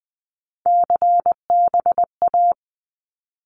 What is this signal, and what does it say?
Morse code, heard as a keyed tone, about 20 wpm, 700 Hz. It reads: CBA
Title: Abbreviations 20wpm